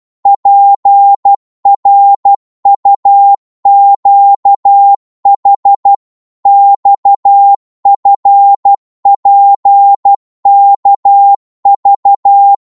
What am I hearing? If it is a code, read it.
Morse code, heard as a keyed tone, about 12 wpm, 800 Hz. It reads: PRUQH XFPKV
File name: Morse / ltr-12wpm